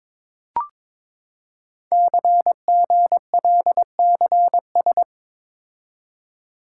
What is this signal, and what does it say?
Morse code, heard as a keyed tone, about 22 wpm, 700 Hz. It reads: CGLCS
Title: Code Groups - 5 Letters - 22wpm